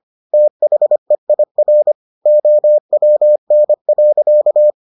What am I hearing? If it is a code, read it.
Morse code, heard as a keyed tone, about 25 wpm, 600 Hz. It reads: THEIR OWN.